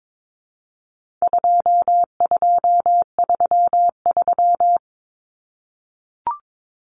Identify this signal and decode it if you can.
Morse code, heard as a keyed tone, about 22 wpm, 700 Hz. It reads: 2233
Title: Common Cut Numbers - 4-Digit 22wpm